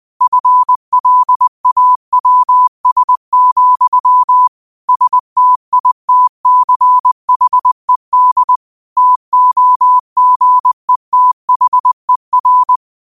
Morse code, heard as a keyed tone, about 20 wpm, 1000 Hz. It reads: FLAWS, STITCHED TOGETHER
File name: qotdfiles / 20wpm